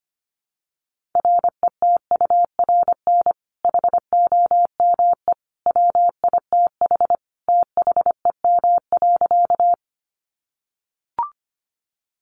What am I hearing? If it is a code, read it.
Morse code, heard as a keyed tone, about 25 wpm, 700 Hz. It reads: RETURN HOME WITH THEM.
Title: Sentences from Top 800 Words 25wpm